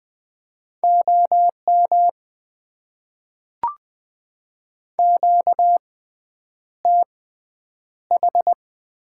Morse code, heard as a keyed tone, about 20 wpm, spaced as for 9 wpm, 700 Hz. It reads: OM QTH
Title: Top 90 QSO Elements - 6x Character Spacing - 20wpm